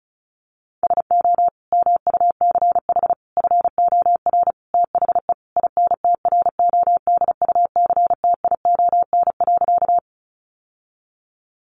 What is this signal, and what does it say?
Morse code, heard as a keyed tone, about 34 wpm, 700 Hz. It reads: SO MUCH FOR THE INTRODUCTION.
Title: Sentences from Top 1000 Words 35wpm